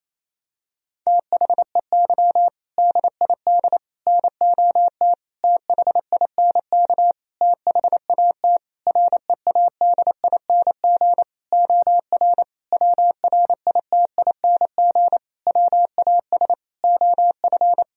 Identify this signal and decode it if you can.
Morse code, heard as a keyed tone, about 27 wpm, 700 Hz. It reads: THEY DID NOT THINK THAT READING OR WRITING WAS OF
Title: Sentences from Top 1000 Words - Review Part 3 - 28wpm